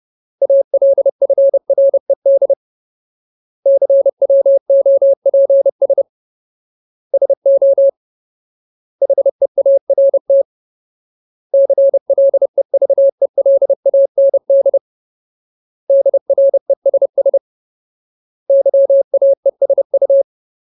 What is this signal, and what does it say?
Morse code, heard as a keyed tone, about 30 wpm, 550 Hz. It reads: ALFRED CWOPS SO HEART CLEVELAND DRESS YAESU
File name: Ultimate Word List / 30WPM Ultimate